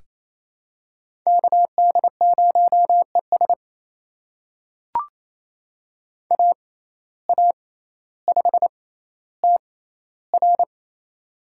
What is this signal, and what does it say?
Morse code, heard as a keyed tone, about 28 wpm, spaced as for 14 wpm, 700 Hz. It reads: KD0ES AA5TR
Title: USA Call Signs - 5 Characters - 6x Character Spacing - 28wpm